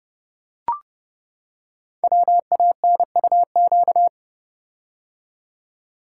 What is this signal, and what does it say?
Morse code, heard as a keyed tone, about 29 wpm, 700 Hz. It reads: WANUQ
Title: Code Groups - 5 Letter:Numbers - 30wpm